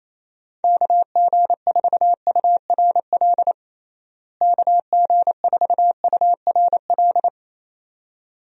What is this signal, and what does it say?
Morse code, heard as a keyed tone, about 28 wpm, 700 Hz. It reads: KG4URL KG4URL
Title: USA Call Signs - 6 Characters - Repeated 2x - 28wpm